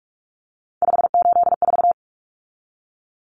Morse code, heard as a keyed tone, about 44 wpm, 700 Hz. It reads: 584
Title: Numbers - 3-Digit 45wpm